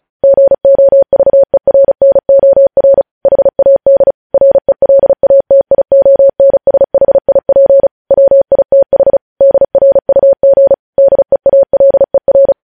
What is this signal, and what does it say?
Morse code, heard as a keyed tone, about 36 wpm, 550 Hz. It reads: GOVERNOR HAD RELATIONSHIP WITH DRUG DEALER
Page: News Headlines in Morse Code at 35 WPM